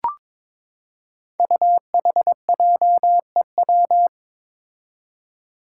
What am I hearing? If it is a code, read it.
Morse code, heard as a keyed tone, about 22 wpm, 700 Hz. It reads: UHJEW
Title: Five Challenging Letter-Number Combinations 22wpm